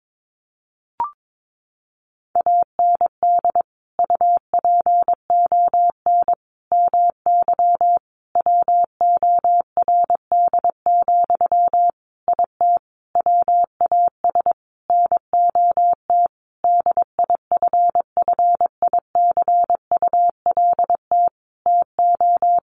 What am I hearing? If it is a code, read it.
Morse code, heard as a keyed tone, about 22 wpm, 700 Hz. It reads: AND UPON MY WORD, IT WAS NOT DIFFICULT TO
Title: Sentences from Top 1100 Words 22wpm